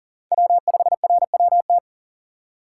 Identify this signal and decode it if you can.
Morse code, heard as a keyed tone, about 40 wpm, 700 Hz. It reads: W5RWT
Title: USA Call Signs - 5 Characters - 6x Character Spacing - 40wpm